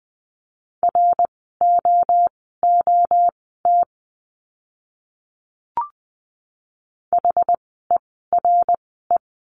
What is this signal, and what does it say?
Morse code, heard as a keyed tone, about 20 wpm, spaced as for 14 wpm, 700 Hz. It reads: ROOT HERE
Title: 068 Lesson 12f - Letter d - Words 20wpm